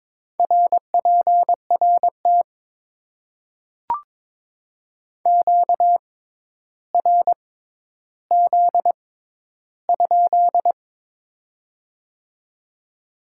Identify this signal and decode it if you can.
Morse code, heard as a keyed tone, about 21 wpm, spaced as for 11 wpm, 700 Hz. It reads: RPRT QRZ?
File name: Top 90 QSO Elements - 6x Character Spacing - 22wpm